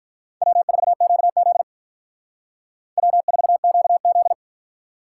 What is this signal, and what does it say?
Morse code, heard as a keyed tone, about 42 wpm, 700 Hz. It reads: W4QZ W4QZ
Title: Speed Racing - USA Call Signs 35wpm